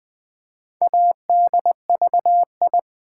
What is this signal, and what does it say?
Morse code, heard as a keyed tone, about 20 wpm, 700 Hz. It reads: ADVI
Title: Build A Word - 2 Letters At A Time - 6-Letter Words - 20wpm